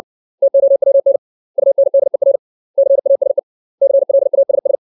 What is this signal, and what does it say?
Morse code, heard as a keyed tone, about 60 wpm, 550 Hz. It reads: TOWN UNDER CASE CLASS
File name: 632 words / 632 Words 60WPM